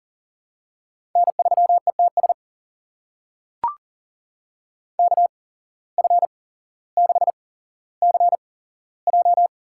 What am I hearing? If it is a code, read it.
Morse code, heard as a keyed tone, about 37 wpm, spaced as for 17 wpm, 700 Hz. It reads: N3ETS KF6CJ
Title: USA Call Signs - 5 Characters - 8x Character Spacing - 40wpm